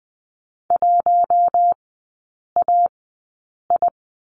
Morse code, heard as a keyed tone, about 20 wpm, spaced as for 8 wpm, 700 Hz. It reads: 1AI
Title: 067 Lesson 12e - Letter d - Three Characters 20wpm